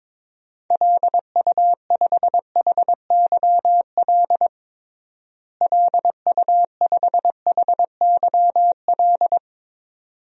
Morse code, heard as a keyed tone, about 22 wpm, 700 Hz. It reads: LU5HYL LU5HYL
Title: DXCC-100 Call Signs - 6 Characters - Repeated 2x - 22wpm